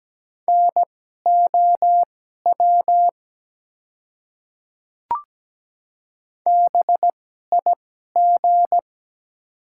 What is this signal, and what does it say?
Morse code, heard as a keyed tone, about 17 wpm, spaced as for 12 wpm, 700 Hz. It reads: NOW BIG
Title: 2x Character Spacing - Top 300 Words - 17wpm